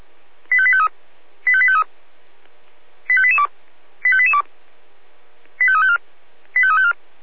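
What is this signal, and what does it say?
Five-tone selective call (ZVEI), heard as quick runs of five tones, tones of about 70 ms. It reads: 85853 85853 85882 85882 85354 85354
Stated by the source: Five-tone ZVEI selective call received by SDR.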